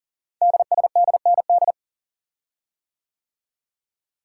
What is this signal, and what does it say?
Morse code, heard as a keyed tone, about 39 wpm, 700 Hz. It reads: DSDND
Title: Code Groups - 5 Letters - 40wpm